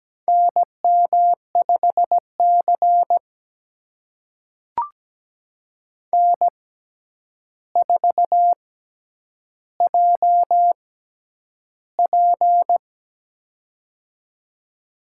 Morse code, heard as a keyed tone, about 17 wpm, spaced as for 9 wpm, 700 Hz. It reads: NM5C N4JP
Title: USA Call Signs - 4 Characters - 6x Character Spacing - 17wpm